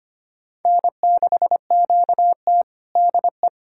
Morse code, heard as a keyed tone, about 25 wpm, 700 Hz. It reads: N6QT DE
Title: USA Callsign DE Callsign - 4 Characters - 25wpm